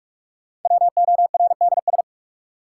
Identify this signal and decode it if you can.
Morse code, heard as a keyed tone, about 43 wpm, 700 Hz. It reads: WORDS
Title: Sentences from Top 1000 Words - Review Part 6 - New Word Drill - 45wpm